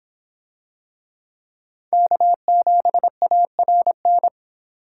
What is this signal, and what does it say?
Morse code, heard as a keyed tone, about 23 wpm, 700 Hz. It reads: K7ARN